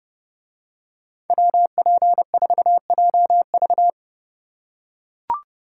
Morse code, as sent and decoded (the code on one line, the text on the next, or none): .-- .--. ....- .--- ...-
WP4JV